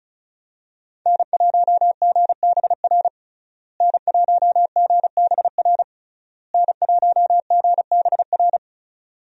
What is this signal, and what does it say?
Morse code, heard as a keyed tone, about 36 wpm, 700 Hz. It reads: N1GBR N1GBR N1GBR
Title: USA Call Signs - 5 Characters - Repeated 3x - 35wpm